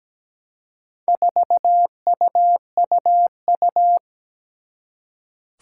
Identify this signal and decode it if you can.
Morse code, heard as a keyed tone, about 17 wpm, 700 Hz. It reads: 4UUU
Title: Four Challenging Letter-Number Combinations 17wpm